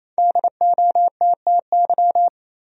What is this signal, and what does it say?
Morse code, heard as a keyed tone, about 28 wpm, 700 Hz. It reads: DOTTY
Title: USA Common Names - Max 5 Letters - 28wpm